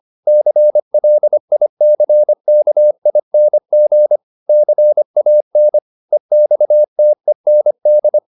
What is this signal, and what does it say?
Morse code, heard as a keyed tone, about 25 wpm, 600 Hz. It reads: CLICKING CAN EXTEND